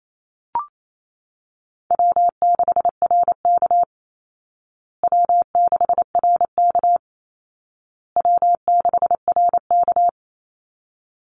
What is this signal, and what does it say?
Morse code, heard as a keyed tone, about 28 wpm, 700 Hz. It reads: W6RK W6RK W6RK
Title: Easy Call Signs Repeated 3x - 28wpm